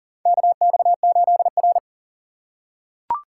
Morse code, heard as a keyed tone, about 40 wpm, 700 Hz. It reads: KX8R
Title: USA Call Signs - 4 Characters - 3x Character Spacing - 40wpm